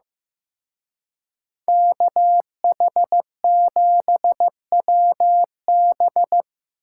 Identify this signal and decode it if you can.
Morse code, heard as a keyed tone, about 18 wpm, 700 Hz. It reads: KH7WB
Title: Speed Racing - USA Call Signs 15wpm